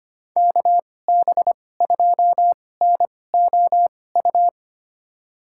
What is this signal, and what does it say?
Morse code, heard as a keyed tone, about 25 wpm, spaced as for 18 wpm, 700 Hz. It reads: KB2NOU
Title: USA Call Signs - 6 Characters - 2x Character Spacing - 25wpm